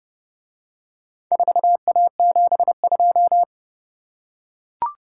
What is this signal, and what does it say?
Morse code, heard as a keyed tone, about 29 wpm, 700 Hz. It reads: 4A72